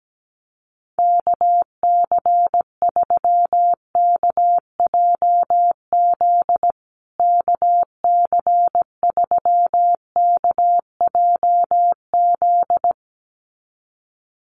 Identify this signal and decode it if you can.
Morse code, heard as a keyed tone, about 17 wpm, 700 Hz. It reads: KC3KJZ KC3KJZ
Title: USA Call Signs Repeated 2x 17wpm